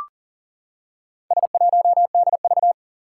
Morse code, heard as a keyed tone, about 40 wpm, 700 Hz. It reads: S1DV
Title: Four Challenging Letter-Number Combinations 40wpm